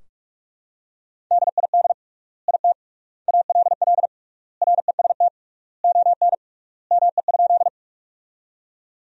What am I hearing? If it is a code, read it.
Morse code, heard as a keyed tone, about 43 wpm, 700 Hz. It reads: DID IT ALL REST ON ME?
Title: 3x Word-Spacing - Sentences from Top 500 Words - 45wpm